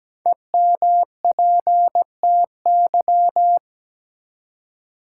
Morse code, heard as a keyed tone, about 17 wpm, 700 Hz. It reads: EMPTY
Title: Five Letter Words - 17wpm